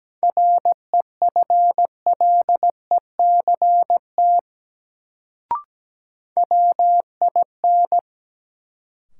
Morse code, heard as a keyed tone, about 17 wpm, 700 Hz. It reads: REFLECT WIN
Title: Sentences from Top 1500 Words - New Word Drill - 17wpm